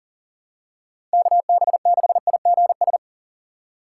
Morse code, heard as a keyed tone, about 40 wpm, 700 Hz. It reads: KB6IGS
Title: USA Call Signs - 6 Characters - 6x Character Spacing - 40wpm